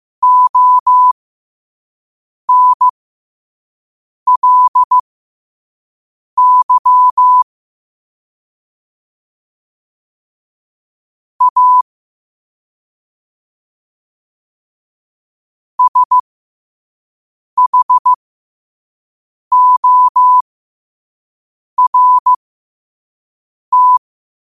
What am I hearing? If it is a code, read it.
Morse code, heard as a keyed tone, about 15 wpm, spaced as for 5 wpm, 1000 Hz. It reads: ONLY A SHORT
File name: qotdfiles / 5wpm